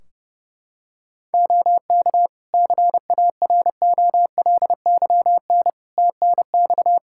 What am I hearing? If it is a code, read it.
Morse code, heard as a keyed tone, about 30 wpm, 700 Hz. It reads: OK CAROLYN TNX
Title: Practice QSO - Exchange 4 - Ending - 30wpm